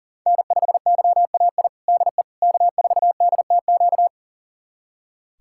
Morse code, heard as a keyed tone, about 37 wpm, 700 Hz. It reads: N5YAI DE K4DTQ
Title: USA Callsign DE Callsign 40wpm